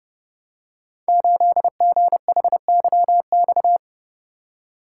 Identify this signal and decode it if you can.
Morse code, heard as a keyed tone, about 30 wpm, 700 Hz. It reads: 8GHYX
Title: Code Groups - 5 Letter:Numbers - 30wpm